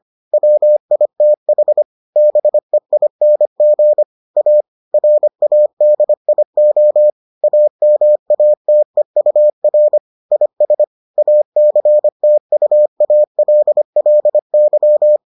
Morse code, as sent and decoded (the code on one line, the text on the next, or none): .-- .. - .... / -... .. -. --. / .- / .-. .- -.. .. --- / .- -- .- - . ..- .-. / .. ... / .- -.-. - ..- .- .-.. .-.. -.--
WITH BEING A RADIO AMATEUR IS ACTUALLY